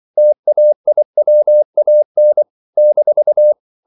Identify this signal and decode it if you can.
Morse code, heard as a keyed tone, about 24 wpm, 600 Hz. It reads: TAIWAN -
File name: CW / Practice / NationalCapitals24